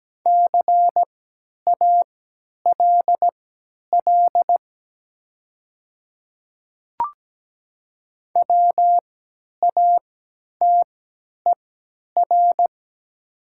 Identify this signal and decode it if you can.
Morse code, heard as a keyed tone, about 17 wpm, spaced as for 10 wpm, 700 Hz. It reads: CALL WATER
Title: 3x Character Spacing - Top 100 Words - 17wpm